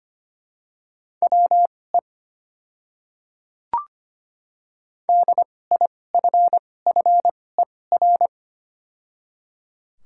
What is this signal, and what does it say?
Morse code, heard as a keyed tone, about 25 wpm, spaced as for 18 wpm, 700 Hz. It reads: WE DIFFER
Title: 2x Character Spacing - Top 300 Words - 25wpm